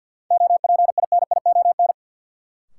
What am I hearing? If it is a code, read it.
Morse code, heard as a keyed tone, about 47 wpm, 700 Hz. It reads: OPINION
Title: Sentences from Top 1200 Words - New Word Drill - 50wpm